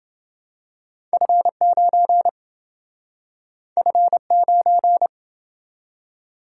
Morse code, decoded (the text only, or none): F9 F9